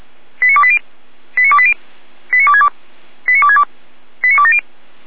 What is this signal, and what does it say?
Five-tone selective call (ZVEI), heard as quick runs of five tones, tones of about 70 ms. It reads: 89280 89280 89262 89262 89270